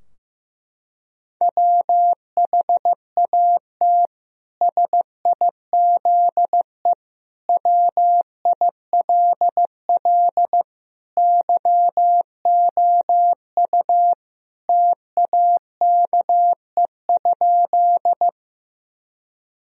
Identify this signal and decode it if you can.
Morse code, heard as a keyed tone, about 15 wpm, 700 Hz. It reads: WHAT SIZE WILL YOU TAKE?
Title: Sentences from Top 1000 Words - Review Part 2 - 15wpm